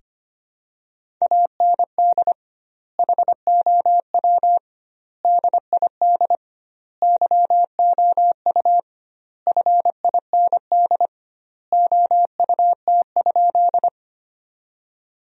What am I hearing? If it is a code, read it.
Morse code, heard as a keyed tone, about 25 wpm, 700 Hz. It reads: AND HOW DID YOU FIND OUT?
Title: 2x Word-Spacing - Sentences from Top 100 Words - 25wpm